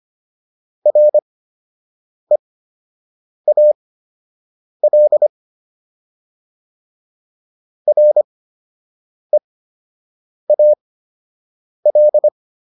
Morse code, heard as a keyed tone, about 25 wpm, spaced as for 7 wpm, 600 Hz. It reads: REAL REAL